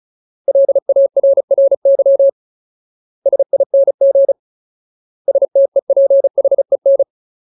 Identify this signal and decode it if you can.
Morse code, heard as a keyed tone, about 34 wpm, 550 Hz. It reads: LARRY SING STEPHEN
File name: Ultimate Word List / 35WPM Ultimate